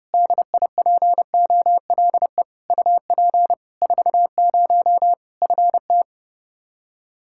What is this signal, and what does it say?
Morse code, heard as a keyed tone, about 30 wpm, 700 Hz. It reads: DIPOLE UP 40 FT
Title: Practice QSO - Exchange 2 - Info - 30wpm